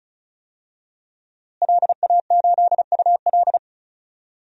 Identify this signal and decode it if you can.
Morse code, heard as a keyed tone, about 36 wpm, 700 Hz. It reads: LA8UL
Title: DXCC-100 Call Signs - 5 Characters - Repeated 2x - 35wpm